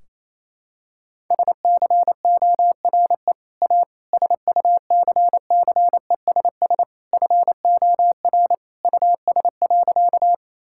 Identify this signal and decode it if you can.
Morse code, heard as a keyed tone, about 28 wpm, 700 Hz. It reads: SCORE A SUCCESS FOR US.
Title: Sentences from Top 1600 Words 28wpm